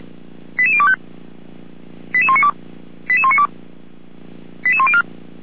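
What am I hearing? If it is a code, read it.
Five-tone selective call (ZVEI), heard as quick runs of five tones, tones of about 70 ms. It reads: 88025 88192 88192 88104
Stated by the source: ZVEI five-tone selective call received with an SDR for Feuerwehr Orlamünde, a fire brigade.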